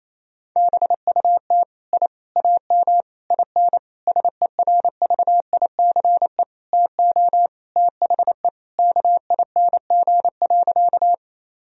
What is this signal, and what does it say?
Morse code, heard as a keyed tone, about 28 wpm, 700 Hz. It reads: BUT I AM IN SERVICE TO THE KING.
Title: Sentences from Top 1000 Words - Review Part 4 - 28wpm